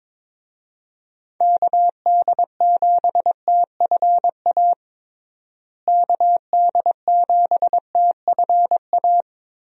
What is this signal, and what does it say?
Morse code, heard as a keyed tone, about 22 wpm, 700 Hz. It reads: KD7TFA KD7TFA
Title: USA Call Signs - 6 Characters - Repeated 3x - 22wpm